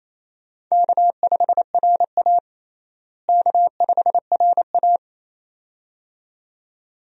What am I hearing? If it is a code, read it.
Morse code, heard as a keyed tone, about 28 wpm, 700 Hz. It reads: K5RA K5RA
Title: USA Call Signs - 4 Characters - Repeated 2x - 28wpm